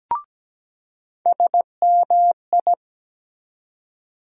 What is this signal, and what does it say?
Morse code, heard as a keyed tone, about 17 wpm, 700 Hz. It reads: SMI
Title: USA Call Sign Postfixes - 3 Letters 17wpm